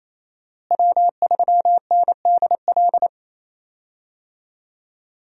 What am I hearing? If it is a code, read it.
Morse code, heard as a keyed tone, about 28 wpm, 700 Hz. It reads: W3NDL
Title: USA Call Signs - 5 Characters - Repeated 3x - 28wpm